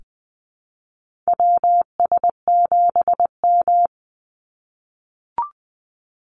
Morse code, heard as a keyed tone, about 20 wpm, 700 Hz. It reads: WS7M
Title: Call Signs - 20wpm